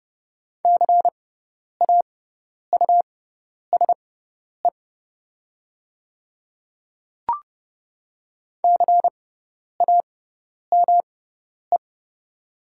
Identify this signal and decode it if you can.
Morse code, heard as a keyed tone, about 30 wpm, spaced as for 10 wpm, 700 Hz. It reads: CAUSE CAME
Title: 6x Character Spacing - Top 200 Words - 30wpm